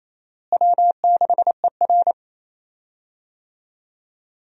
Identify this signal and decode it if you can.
Morse code, heard as a keyed tone, about 27 wpm, 700 Hz. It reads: W6ER
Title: USA Call Signs - 4 Characters - Repeated 2x - 28wpm